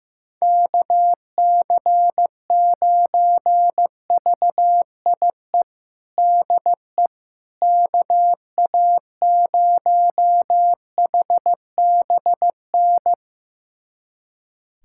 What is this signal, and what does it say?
Morse code, heard as a keyed tone, about 15 wpm, 700 Hz. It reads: KC9VIE DE KA0HBN
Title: USA Callsign DE Callsign - 6 Characters - 15wpm